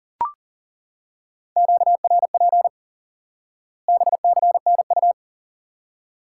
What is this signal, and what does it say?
Morse code, heard as a keyed tone, about 40 wpm, 700 Hz. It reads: QRP BCNU
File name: IQE Course 3 - Lesson 89 - YAGI - Two Elements - 40wpm